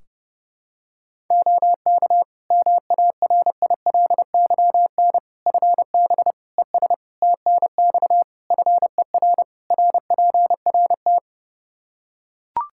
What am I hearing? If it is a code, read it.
Morse code, heard as a keyed tone, about 30 wpm, 700 Hz. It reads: OK MARILYN FB ES TNX FER RPRT
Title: Practice QSO - Exchange 2 - Info - 30wpm